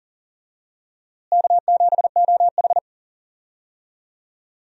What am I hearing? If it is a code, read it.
Morse code, heard as a keyed tone, about 40 wpm, 700 Hz. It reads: K7OH
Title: USA Call Signs - 4 Characters - Repeated 3x - 40wpm